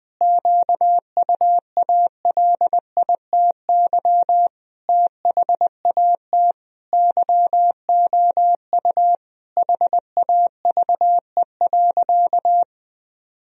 Morse code, heard as a keyed tone, about 20 wpm, 700 Hz. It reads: QUALITY THAT YOU HAVE.
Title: Sentences from Top 1000 Words - Review Part 4 - 20wpm